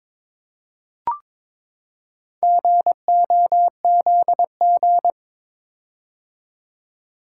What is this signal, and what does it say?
Morse code, heard as a keyed tone, about 22 wpm, 700 Hz. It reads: GOZG
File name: practice / 22wpm / Four Letter-Number Combinations - 22wpm